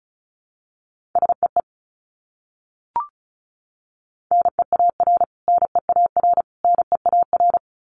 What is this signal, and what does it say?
Morse code, heard as a keyed tone, about 34 wpm, 700 Hz. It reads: SEE NEAR NEAR NEAR
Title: Top 300 Words Repeated 35wpm